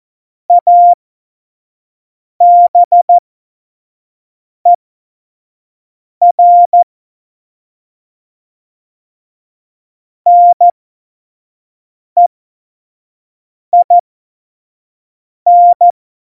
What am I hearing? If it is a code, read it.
Morse code, heard as a keyed tone, about 14 wpm, spaced as for 5 wpm, 700 Hz. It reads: ABER NEIN
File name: Morsekurs / Witze / Witz 04. September 2018, 25 F70